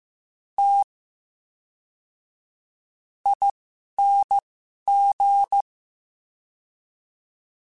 Morse code, heard as a keyed tone, about 15 wpm, spaced as for 11 wpm, 800 Hz. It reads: T ING
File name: Special INGs / Special INGs-mp3s / VerySlowToFaster Ing @10